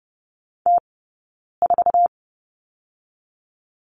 Morse code, heard as a keyed tone, about 30 wpm, 700 Hz. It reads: T 4
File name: Lesson 18d - Letter m - Two Characters 30wpm